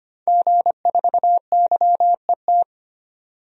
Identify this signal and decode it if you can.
Morse code, heard as a keyed tone, about 25 wpm, 700 Hz. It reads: G4YET
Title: DXCC-100 Call Signs - 5 Characters - 25wpm